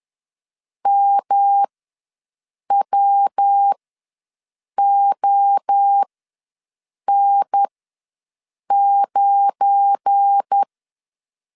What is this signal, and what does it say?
Morse code, heard as a keyed tone, about 10 wpm, spaced as for 6 wpm, 800 Hz. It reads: MWON9
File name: LEZIONI / 7 GF9Z / 7 GF9 ws 10wpm 4034